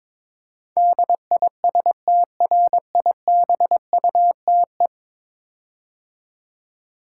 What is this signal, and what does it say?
Morse code, heard as a keyed tone, about 22 wpm, 700 Hz. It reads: DISTRIBUTE
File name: practice / 22wpm / 10 Letter Words - First 3 Letters Given - 22wpm